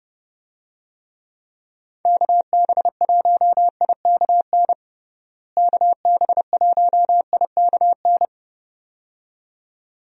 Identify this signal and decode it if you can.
Morse code, heard as a keyed tone, about 29 wpm, 700 Hz. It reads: KB1IKN KB1IKN
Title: USA Call Signs - 6 Characters - Repeated 2x - 30wpm